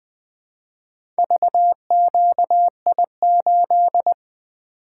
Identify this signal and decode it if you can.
Morse code, heard as a keyed tone, about 20 wpm, 700 Hz. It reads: VQI8
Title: Four Letter-Number Combinations - 20wpm